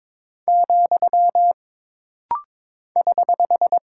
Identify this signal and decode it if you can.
Morse code, heard as a keyed tone, about 22 wpm, 700 Hz. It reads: , <HH>
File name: practice / 22wpm / Single Punctuation-Prosign - Rapid-Fire - 22wpm